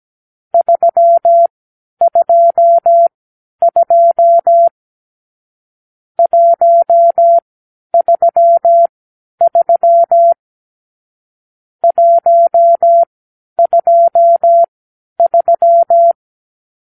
Morse code, heard as a keyed tone, about 17 wpm, spaced as for 10 wpm, 680 Hz. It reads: 322 133 123
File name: k7qo / Lesson 44 (1-3)